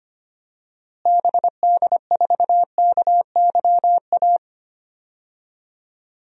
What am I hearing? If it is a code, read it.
Morse code, heard as a keyed tone, about 25 wpm, 700 Hz. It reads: BD4KYA
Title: DXCC-100 Call Signs - 6 Characters - Repeated 2x - 25wpm